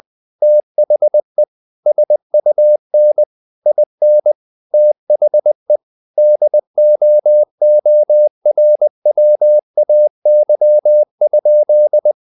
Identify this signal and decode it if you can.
Morse code, heard as a keyed tone, about 20 wpm, 600 Hz. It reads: THE SUN IN THE DOORWAY?
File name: the old man and the sea / the old man and the sea part 1 20wpm